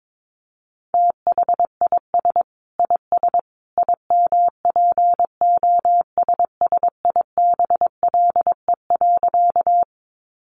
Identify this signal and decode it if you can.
Morse code, heard as a keyed tone, about 22 wpm, 700 Hz. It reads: THIS IS IMPOSSIBLE.
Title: A Sound Of Thunder 22wpm